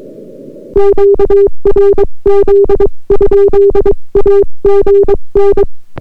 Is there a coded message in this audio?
Yes: Morse code from 0.8 to 5.6 s, heard as a keyed tone, about 22 wpm, 400 Hz. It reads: QRZ? AGN